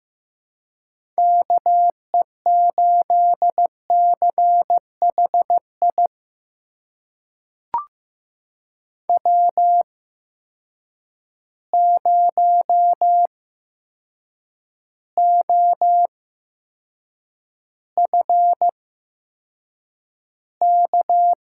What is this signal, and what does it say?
Morse code, heard as a keyed tone, about 15 wpm, spaced as for 7 wpm, 700 Hz. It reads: KE8CHI W0OFK
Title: USA Call Signs - 8x Character Spacing - 15wpm